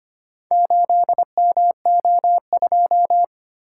Morse code, heard as a keyed tone, about 25 wpm, 690 Hz. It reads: 8MO2
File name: practice / 25wpm / Four Letter-Number Combinations - 25wpm